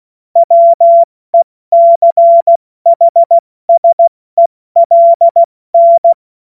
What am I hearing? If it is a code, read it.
Morse code, heard as a keyed tone, about 16 wpm, spaced as for 14 wpm, 680 Hz. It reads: WECHSELN
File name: Morsekurs / Witze / Witz 30. August 2022 70 F80